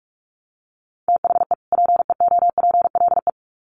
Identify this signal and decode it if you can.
Morse code, heard as a keyed tone, about 44 wpm, 700 Hz. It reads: THE PEOPLE
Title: Sets of 2 Words 45wpm